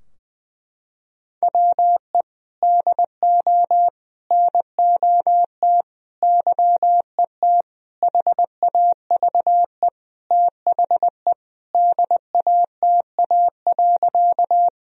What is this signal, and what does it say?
Morse code, heard as a keyed tone, about 20 wpm, 700 Hz. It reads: WE DO NOT YET HAVE THE DATA.
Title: Sentences from Top 1000 Words - Review Part 3 - 20wpm